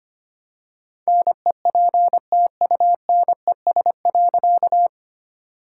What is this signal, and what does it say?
Morse code, heard as a keyed tone, about 25 wpm, 700 Hz. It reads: NEPTUNES.
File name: practice / 25wpm / All-Age Appropriate Jokes - 25wpm